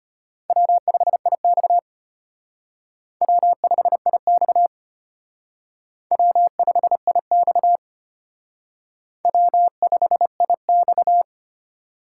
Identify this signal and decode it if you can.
Morse code, heard as a keyed tone, about 33 wpm, 700 Hz. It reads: W5IX W5IX W5IX W5IX